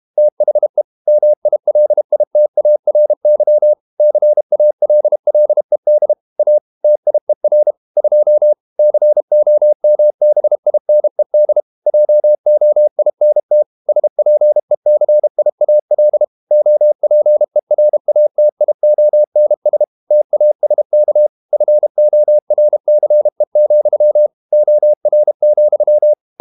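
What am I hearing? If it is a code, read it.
Morse code, heard as a keyed tone, about 32 wpm, 600 Hz. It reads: THE MILITARY CALLED A TIER 2 COMBINED JOINT SPECIAL OPERATIONS TASK FORCE, OR,
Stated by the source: Morse code practice at 32 wpm from CW med Gnister, reading Ken Follett's novel Never in English.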